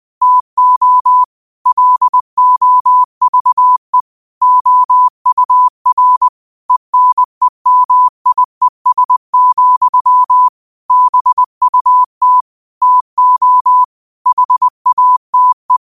Morse code, heard as a keyed tone, about 20 wpm, 1000 Hz. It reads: TO LOVE OUR ENEMIES, BUT TO HATE
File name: qotdfiles / 20wpm